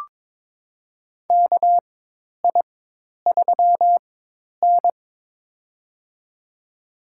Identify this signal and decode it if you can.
Morse code, heard as a keyed tone, about 22 wpm, spaced as for 10 wpm, 700 Hz. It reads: KI3N
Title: USA Call Signs - 4x Character Spacing - 22wpm